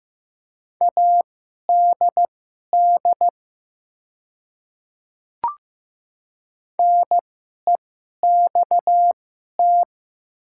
Morse code, heard as a keyed tone, about 15 wpm, spaced as for 11 wpm, 700 Hz. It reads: ADD NEXT
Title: 2x Character Spacing - Top 500 Words - 15wpm